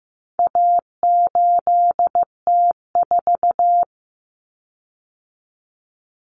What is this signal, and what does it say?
Morse code, heard as a keyed tone, about 15 wpm, 700 Hz. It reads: A8T4
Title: Common Cut Numbers - 4-Digit 15wpm